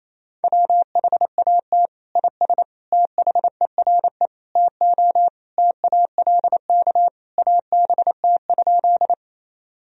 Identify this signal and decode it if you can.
Morse code, heard as a keyed tone, about 28 wpm, 700 Hz. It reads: WHAT IS THERE TO TALK ABT?
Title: Sentences from Top 400 Words and Abbreviations 28wpm